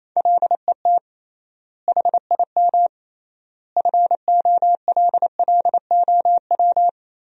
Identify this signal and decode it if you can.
Morse code, heard as a keyed tone, about 28 wpm, 700 Hz. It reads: LET HIM FOLLOW
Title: 3x Word-Spacing - Sentences from Top 300 Words - 28wpm